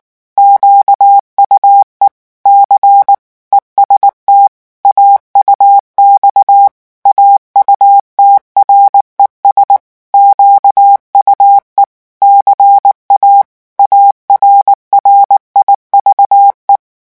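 Morse code, heard as a keyed tone, about 19 wpm, 800 Hz. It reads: QUE C EST AUX AUTRES QUE CA ARRIVE
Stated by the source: French Morse revision 8 drill of random five-letter groups at 18 words per minute.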